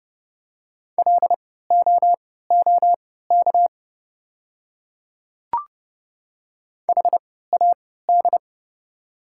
Morse code, heard as a keyed tone, about 30 wpm, spaced as for 17 wpm, 700 Hz. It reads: LOOK HAD
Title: 3x Character Spacing - Top 100 Words - 30wpm